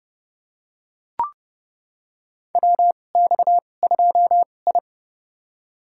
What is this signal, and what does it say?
Morse code, heard as a keyed tone, about 30 wpm, spaced as for 22 wpm, 700 Hz. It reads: WX2I
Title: USA Call Signs - 4 Characters - 2x Character Spacing - 30wpm